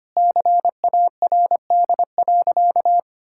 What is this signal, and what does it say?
Morse code, heard as a keyed tone, about 25 wpm, 700 Hz. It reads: CARD.
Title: Sentences from Top 1000 Words - Review Part 6 - 25wpm